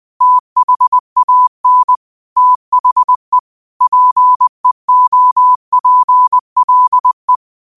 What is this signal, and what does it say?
Morse code, heard as a keyed tone, about 20 wpm, 1000 Hz. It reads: THAN THE PEOPLE
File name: qotdfiles / 20wpm